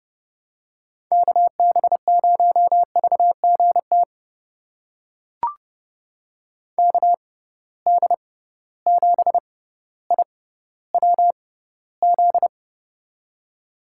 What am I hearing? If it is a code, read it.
Morse code, heard as a keyed tone, about 31 wpm, spaced as for 15 wpm, 700 Hz. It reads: KB0VGT KD7IWZ
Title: USA Call Signs - 6 Characters - 6x Character Spacing - 30wpm